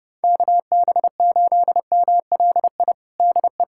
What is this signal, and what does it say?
Morse code, heard as a keyed tone, about 30 wpm, 700 Hz. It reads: KB8MLI DE
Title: USA Callsign DE Callsign 30wpm